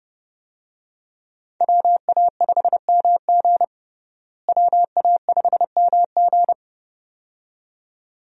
Morse code, heard as a keyed tone, about 32 wpm, 700 Hz. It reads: WA5MG WA5MG
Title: USA Call Signs - 5 Characters - Repeated 2x - 30wpm